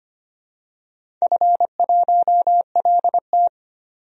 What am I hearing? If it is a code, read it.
Morse code, heard as a keyed tone, about 25 wpm, 700 Hz. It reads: F1LT
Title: DXCC-100 Call Signs - 4 Characters - 25wpm